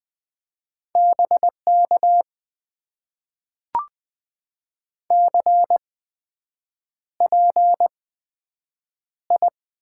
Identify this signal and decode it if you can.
Morse code, heard as a keyed tone, about 20 wpm, spaced as for 7 wpm, 700 Hz. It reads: BK CPI
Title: Top 90 QSO Elements - 8x Character Spacing - 20wpm